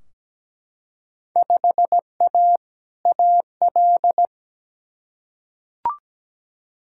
Morse code, heard as a keyed tone, about 17 wpm, 700 Hz. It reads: 5A AL
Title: Field Day Exchanges Encore - 17wpm